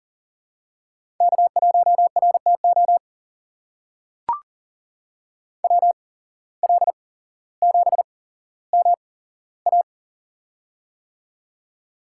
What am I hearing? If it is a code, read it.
Morse code, heard as a keyed tone, about 40 wpm, spaced as for 17 wpm, 700 Hz. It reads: K1RTO WL7MA
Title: USA Call Signs - 5 Characters - 8x Character Spacing - 40wpm